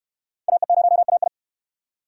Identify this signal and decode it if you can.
Morse code, heard as a keyed tone, about 69 wpm, 700 Hz. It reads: AE0AI